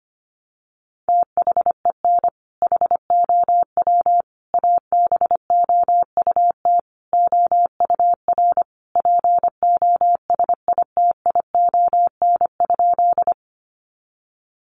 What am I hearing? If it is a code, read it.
Morse code, heard as a keyed tone, about 25 wpm, 700 Hz. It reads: THEN HOW ABOUT OUR POSITION?
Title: Sentences from Top 900 Words 25wpm